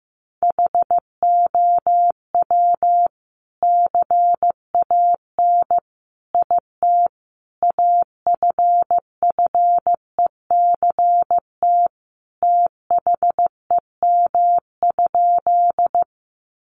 Morse code, heard as a keyed tone, about 15 wpm, 700 Hz. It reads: HOW CAN IT AFFECT THEM?
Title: Sentences from Top 900 Words 15wpm